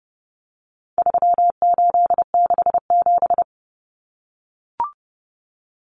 Morse code, heard as a keyed tone, about 30 wpm, 700 Hz. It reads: 3867
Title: Numbers - 4-Digit 30wpm